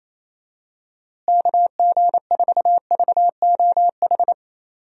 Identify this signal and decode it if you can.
Morse code, heard as a keyed tone, about 28 wpm, 700 Hz. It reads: KG4VOH
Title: USA Call Signs - 6 Characters - Repeated 3x - 28wpm